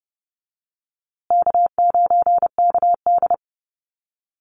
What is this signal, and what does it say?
Morse code, heard as a keyed tone, about 30 wpm, 700 Hz. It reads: K9KD
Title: USA Call Signs Repeated 3x 30wpm